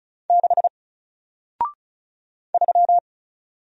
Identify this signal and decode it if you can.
Morse code, heard as a keyed tone, about 36 wpm, 700 Hz. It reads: 6 3
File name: Single-Letter-Number-35wpm